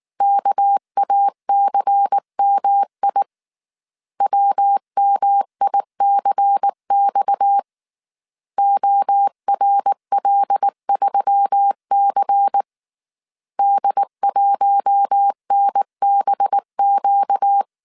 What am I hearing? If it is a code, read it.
Morse code, heard as a keyed tone, about 19 wpm, 800 Hz. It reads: KACMI WMICX ORL3C D1NBQ